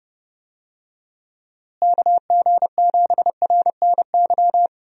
Morse code, heard as a keyed tone, about 30 wpm, 700 Hz. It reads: KG7RNY
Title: USA Call Signs - 6 Characters - Repeated 3x - 30wpm